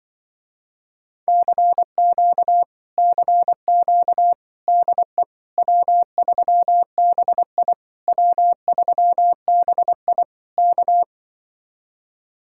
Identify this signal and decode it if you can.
Morse code, heard as a keyed tone, about 25 wpm, 700 Hz. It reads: CQ CQ DE W3BI W3BI K